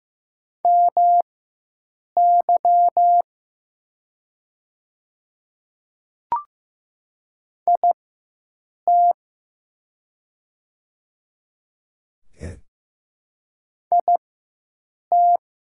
Morse code, heard as a keyed tone, about 15 wpm, spaced as for 7 wpm, 700 Hz. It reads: MY IT IT
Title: 4x Character Spacing - Top 100 Words - 15wpm